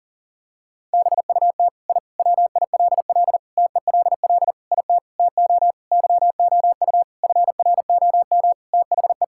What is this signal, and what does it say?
Morse code, heard as a keyed tone, about 40 wpm, 700 Hz. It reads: BUT I WILL TELL IT TO YOU FROM THE
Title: Sentences from Top 1000 Words - Review Part 5 - 40wpm